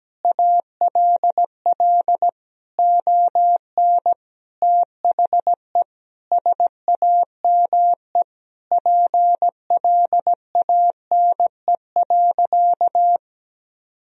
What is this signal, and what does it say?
Morse code, heard as a keyed tone, about 17 wpm, 700 Hz. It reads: ALL ON THE SAME PLANE.
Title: Sentences from Top 1000 Words - Review Part 3 - 17wpm